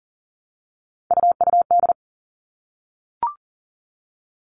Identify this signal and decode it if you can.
Morse code, heard as a keyed tone, about 40 wpm, 700 Hz. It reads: UUD
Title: All Cut Numbers - 3-Digit 40wpm